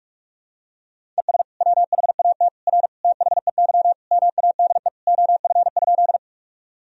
Morse code, heard as a keyed tone, about 45 wpm, 700 Hz. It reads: ES WHAT R THEY MADE OF?
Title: Sentences from Top 200 Words and Abbreviations 45wpm